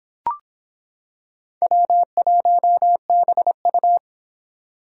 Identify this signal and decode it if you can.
Morse code, heard as a keyed tone, about 26 wpm, 700 Hz. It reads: W1BU